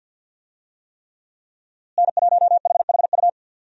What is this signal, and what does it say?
Morse code, heard as a keyed tone, about 47 wpm, 700 Hz. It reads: N1HHU
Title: USA Call Signs - 5 Characters - Repeated 3x - 50wpm